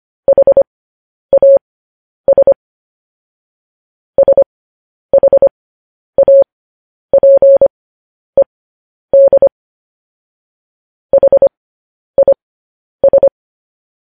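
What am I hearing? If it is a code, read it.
Morse code, heard as a keyed tone, about 25 wpm, spaced as for 10 wpm, 550 Hz. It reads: HAS SHAPED HIS
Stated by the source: Morse code news headlines broadcast at 10 wpm.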